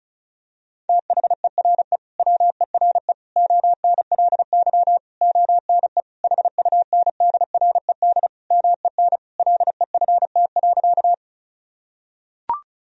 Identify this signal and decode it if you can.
Morse code, heard as a keyed tone, about 36 wpm, 700 Hz. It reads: THERE WERE ONLY ONE HUNDRED MEN LEFT.
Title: Sentences from Top 1000 Words - Review Part 2 - 35wpm